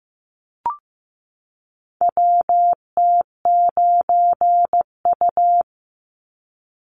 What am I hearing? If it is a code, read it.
Morse code, heard as a keyed tone, about 15 wpm, 700 Hz. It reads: WT9U
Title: Call Signs Repeated 3x - 15wpm